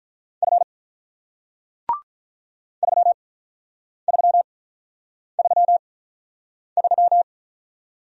Morse code, heard as a keyed tone, about 48 wpm, spaced as for 9 wpm, 690 Hz. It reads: F 3333